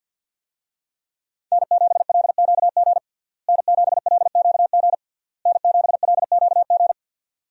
Morse code, heard as a keyed tone, about 52 wpm, 700 Hz. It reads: N7LQG N7LQG N7LQG
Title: USA Call Signs - 5 Characters - Repeated 3x - 50wpm